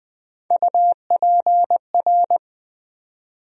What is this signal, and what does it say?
Morse code, heard as a keyed tone, about 20 wpm, 700 Hz. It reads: UPR
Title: USA Call Sign Postfixes - 3 Letters 20wpm